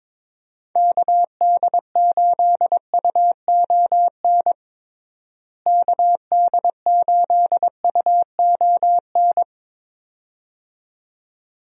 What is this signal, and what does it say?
Morse code, heard as a keyed tone, about 21 wpm, 700 Hz. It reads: KD8UON KD8UON
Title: USA Call Signs - 6 Characters - Repeated 2x - 22wpm